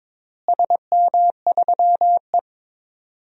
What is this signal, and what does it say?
Morse code, heard as a keyed tone, about 22 wpm, 700 Hz. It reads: SM3E
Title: DXCC-100 Call Signs - 4 Characters - Repeated 3x - 22wpm